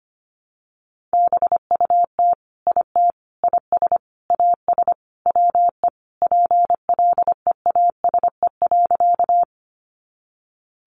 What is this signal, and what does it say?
Morse code, heard as a keyed tone, about 25 wpm, 700 Hz. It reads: BUT IT IS AS WE PLEASE.
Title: Sentences from Top 1500 Words 25wpm